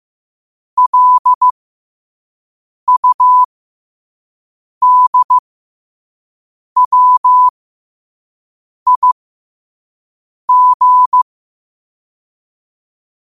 Morse code, heard as a keyed tone, about 15 wpm, spaced as for 5 wpm, 1000 Hz. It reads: LUDWIG
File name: 5wpm